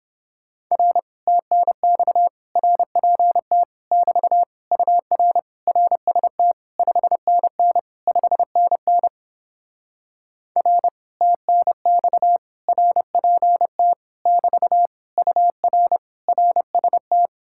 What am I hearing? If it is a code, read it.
Morse code, heard as a keyed tone, about 31 wpm, 700 Hz. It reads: R TNX RPT <BT> UR RST 5NN 5NN R TNX RPT <BT> UR RST